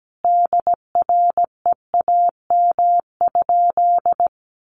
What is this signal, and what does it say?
Morse code, heard as a keyed tone, about 17 wpm, 700 Hz. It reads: DREAM?
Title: Sentences from Top 1500 Words 17wpm